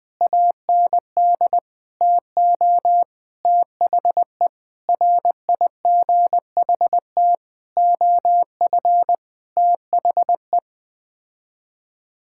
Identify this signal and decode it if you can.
Morse code, heard as a keyed tone, about 21 wpm, 700 Hz. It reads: AND TO THE RIGHT OF THE
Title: Sets of 9 Words - 20wpm